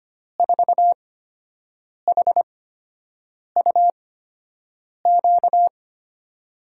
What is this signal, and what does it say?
Morse code, heard as a keyed tone, about 25 wpm, spaced as for 7 wpm, 700 Hz. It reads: 4HUQ
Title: USA Call Signs - 6 Characters - 8x Character Spacing - 25wpm